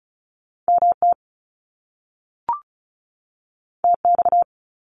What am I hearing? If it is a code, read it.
Morse code, heard as a keyed tone, about 35 wpm, 700 Hz. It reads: MT TX
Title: US States 35wpm